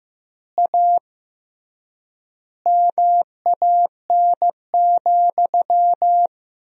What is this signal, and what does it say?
Morse code, heard as a keyed tone, about 15 wpm, 700 Hz. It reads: A MAN,
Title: 3x Word-Spacing - Sentences from Top 200 Words - 15wpm